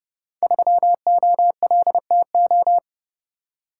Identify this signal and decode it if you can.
Morse code, heard as a keyed tone, about 31 wpm, 700 Hz. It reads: 3OLTO
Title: Code Groups - 5 Letter:Numbers - 30wpm